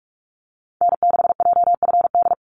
Morse code, heard as a keyed tone, about 46 wpm, 700 Hz. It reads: N6JFD
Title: NA SOTA Activator Callsigns 45wpm